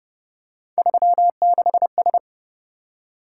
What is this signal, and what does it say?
Morse code, heard as a keyed tone, about 28 wpm, 700 Hz. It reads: 36S